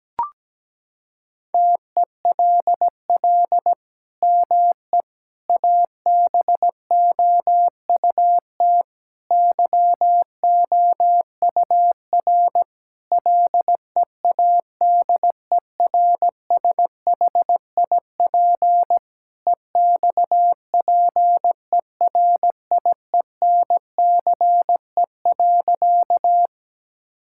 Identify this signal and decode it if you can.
Morse code, heard as a keyed tone, about 17 wpm, 700 Hz. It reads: TELL ME ABOUT YOUR LEADERSHIP EXPERIENCE.
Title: Sentences from Top 1600 Words 17wpm